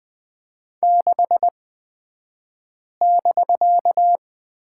Morse code, heard as a keyed tone, about 20 wpm, spaced as for 5 wpm, 700 Hz. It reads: 6<BK>
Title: 249 Lesson 40d - BK - Two Characters 20wpm